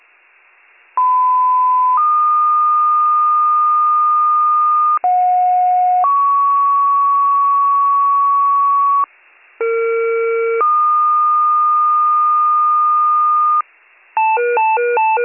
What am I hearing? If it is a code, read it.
Two-tone page, heard as a one-second tone, then a three-second tone, about 1010 then 1220 Hz, about 730 then 1120 Hz, about 460 then 1220 Hz. It is heard from 1.0 to 13.6 s.